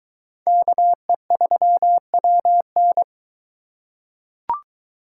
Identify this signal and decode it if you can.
Morse code, heard as a keyed tone, about 21 wpm, 700 Hz. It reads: KE3WN